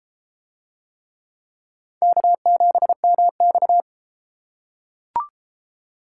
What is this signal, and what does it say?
Morse code, heard as a keyed tone, about 29 wpm, 700 Hz. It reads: K7MX